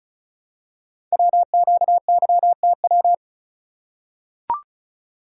Morse code, heard as a keyed tone, about 36 wpm, 700 Hz. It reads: WQYTW
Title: Code Groups - 5 Letters - 35wpm